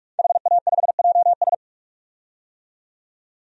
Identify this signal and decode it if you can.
Morse code, heard as a keyed tone, about 44 wpm, 700 Hz. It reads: HA5JS
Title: DXCC-100 Call Signs - 5 Characters - 45wpm